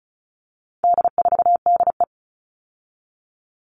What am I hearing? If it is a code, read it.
Morse code, heard as a keyed tone, about 34 wpm, 700 Hz. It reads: D4DE
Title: All Cut Numbers - 4-Digit 35wpm